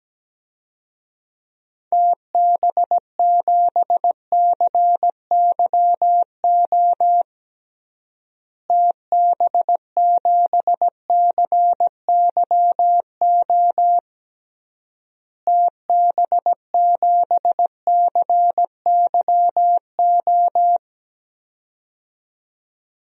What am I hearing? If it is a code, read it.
Morse code, heard as a keyed tone, about 17 wpm, 700 Hz. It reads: TB7CYO TB7CYO TB7CYO
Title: DXCC-100 Call Signs - 6 Characters - Repeated 3x - 17wpm